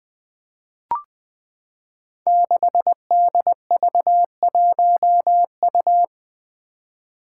Morse code, heard as a keyed tone, about 20 wpm, 700 Hz. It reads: 6DV1U
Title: Five Challenging Letter-Number Combinations 20wpm